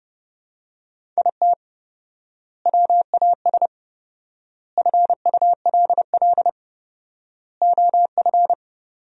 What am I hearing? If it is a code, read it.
Morse code, heard as a keyed tone, about 30 wpm, 700 Hz. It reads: IT WAS FULL OF
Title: 4x Word-Spacing - Sentences from Top 500 Words - 30wpm